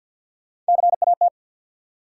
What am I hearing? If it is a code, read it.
Morse code, heard as a keyed tone, about 53 wpm, 700 Hz. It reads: CAT
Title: Three Letter Words - 50wpm